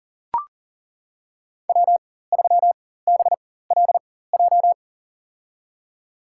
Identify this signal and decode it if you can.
Morse code, heard as a keyed tone, about 40 wpm, spaced as for 19 wpm, 700 Hz. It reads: W3BLJ
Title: USA Call Signs - 5 Characters - 4x Character Spacing - 40wpm